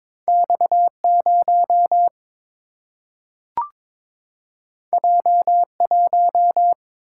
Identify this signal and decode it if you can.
Morse code, heard as a keyed tone, about 23 wpm, 700 Hz. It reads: X0 J1
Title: Single-Letter Single-Number 22wpm